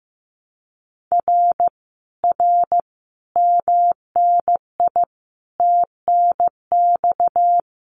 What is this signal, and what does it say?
Morse code, heard as a keyed tone, about 15 wpm, 700 Hz. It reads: R R MNI TNX
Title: Common QSO Phrases Encore 15wpm